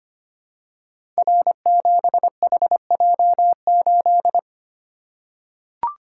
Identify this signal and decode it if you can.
Morse code, heard as a keyed tone, about 25 wpm, 700 Hz. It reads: R7HJ8
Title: Code Groups - 5 Letter:Numbers - 25wpm